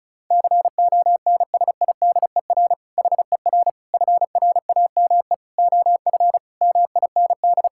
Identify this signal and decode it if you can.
Morse code, heard as a keyed tone, about 36 wpm, 690 Hz. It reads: CONSIDER HER FRAME OF MIND
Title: Sentences from Top 1000 Words - Review Part 6 - 35wpm